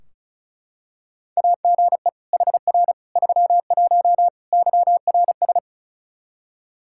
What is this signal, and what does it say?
Morse code, heard as a keyed tone, about 36 wpm, 700 Hz. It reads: AGE HR 31 YRS
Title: Practice QSO - Exchange 3 - More Info - 35wpm